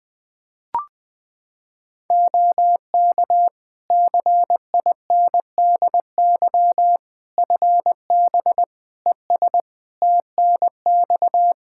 Morse code, heard as a keyed tone, about 21 wpm, 700 Hz. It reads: OK CINDY FB ES TNX
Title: K1USN SST QSO Practice 20wpm